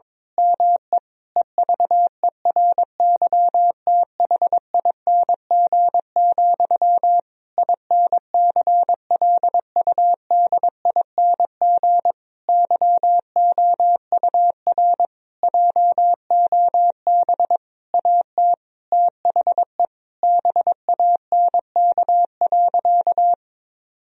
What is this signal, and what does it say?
Morse code, heard as a keyed tone, about 22 wpm, 700 Hz. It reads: ME EVERYTHING, INCLUDING YOUR JOB AT THE BANK.
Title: Sentences from Top 1000 Words - Review Part 4 - 22wpm